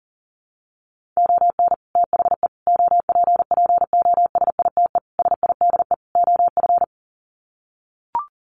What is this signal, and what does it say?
Morse code, heard as a keyed tone, about 40 wpm, 700 Hz. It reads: ON THE OPPOSITE SIDE OF
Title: Sets of 5 Words Encore2 40wpm